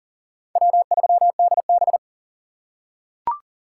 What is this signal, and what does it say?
Morse code, heard as a keyed tone, about 40 wpm, 700 Hz. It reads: W3DB
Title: USA Call Signs - 6x Character Spacing - 40wpm